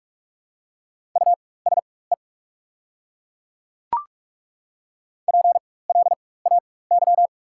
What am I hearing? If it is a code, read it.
Morse code, heard as a keyed tone, about 42 wpm, spaced as for 20 wpm, 690 Hz. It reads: USE PLAY
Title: 4x Character Spacing - Top 300 Words - 45wpm